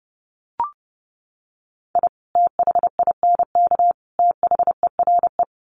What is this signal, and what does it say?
Morse code, heard as a keyed tone, about 28 wpm, 700 Hz. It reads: I THINK THERE
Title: Sets of 3 Words 30wpm